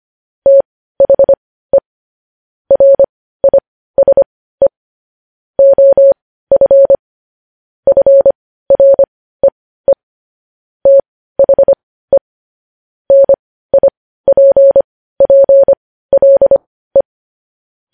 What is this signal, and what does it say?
Morse code, heard as a keyed tone, about 25 wpm, spaced as for 15 wpm, 550 Hz. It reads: THE RISE OF FREE THE NIPPLE
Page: News Headlines in Morse Code at 15 WPM